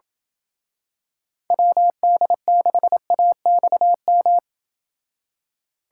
Morse code, heard as a keyed tone, about 27 wpm, 700 Hz. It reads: WD6AXM